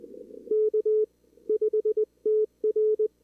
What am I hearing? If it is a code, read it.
Morse code, heard as a keyed tone, about 18 wpm, spaced as for 16 wpm, 440 Hz. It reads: K5TR
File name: CWSprint / Feb2022 / k5tr-feb-na-sprint-cw-k5ot -0000